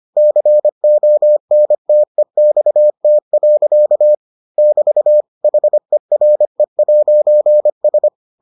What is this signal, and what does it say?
Morse code, heard as a keyed tone, about 25 wpm, 600 Hz. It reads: CONTEXT. <BT> HERE'S